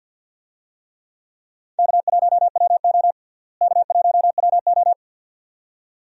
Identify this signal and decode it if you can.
Morse code, heard as a keyed tone, about 53 wpm, 700 Hz. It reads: K1WO K1WO
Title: USA Call Signs - 4 Characters - Repeated 2x - 50wpm